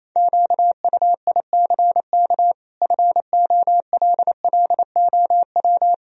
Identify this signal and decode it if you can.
Morse code, heard as a keyed tone, about 28 wpm, 700 Hz. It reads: QUICK FOLLOW
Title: Sentences from Top 1000 Words - Review Part 1 - 28wpm